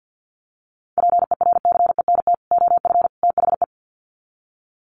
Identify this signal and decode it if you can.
Morse code, heard as a keyed tone, about 50 wpm, 700 Hz. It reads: PERCENT OF THE
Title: Sets of 3 Words 50wpm